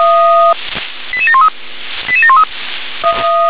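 Five-tone selective call (ZVEI), heard as quick runs of five tones, tones of about 70 ms. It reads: 88713 88713
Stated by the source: ZVEI five-tone selective call received on SDR.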